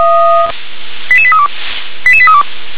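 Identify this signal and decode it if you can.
Five-tone selective call (ZVEI), heard as quick runs of five tones, tones of about 70 ms. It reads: 88832 88832
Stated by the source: Five-tone ZVEI selective call 88771 received by SDR.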